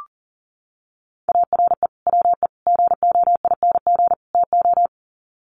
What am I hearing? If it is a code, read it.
Morse code, heard as a keyed tone, about 40 wpm, 700 Hz. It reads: ARE WE GOING TO
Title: Sets of 4 Words 40wpm